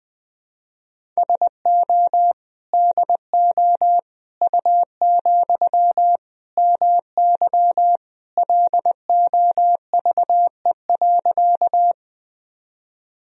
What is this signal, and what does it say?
Morse code, heard as a keyed tone, about 21 wpm, 700 Hz. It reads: SO DO U, MY LOVE.
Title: Sentences from Top 400 Words and Abbreviations 20wpm